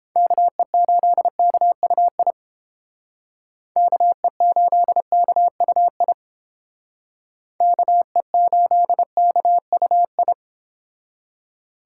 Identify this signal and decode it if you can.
Morse code, heard as a keyed tone, about 30 wpm, 700 Hz. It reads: KE8KUI KE8KUI KE8KUI